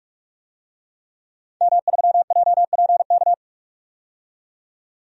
Morse code, heard as a keyed tone, about 46 wpm, 700 Hz. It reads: M3JPK
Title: DXCC-100 Call Signs - 5 Characters - 45wpm